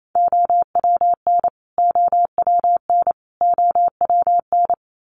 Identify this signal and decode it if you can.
Morse code, heard as a keyed tone, about 28 wpm, 700 Hz. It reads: OWN OWN OWN
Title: Top 300 Words Repeated 28wpm